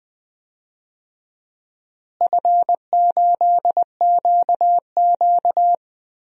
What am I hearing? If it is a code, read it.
Morse code, heard as a keyed tone, about 19 wpm, 690 Hz. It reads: F8QQ
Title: DXCC-100 Call Signs - 4 Characters - Repeated 3x - 20wpm